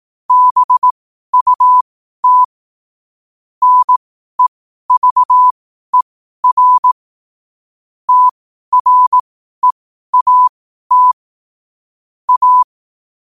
Morse code, heard as a keyed tone, about 18 wpm, spaced as for 12 wpm, 1000 Hz. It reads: BUT NEVER TREAT A
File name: qotdfiles / 12wpm